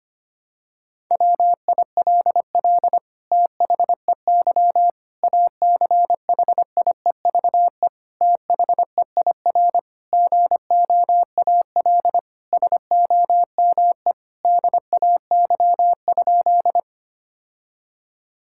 Morse code, heard as a keyed tone, about 25 wpm, 700 Hz. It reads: WILL THEY ACHIEVE THEIR GOAL SOME DAY?
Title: Sentences from Top 1000 Words - Review Part 6 - 25wpm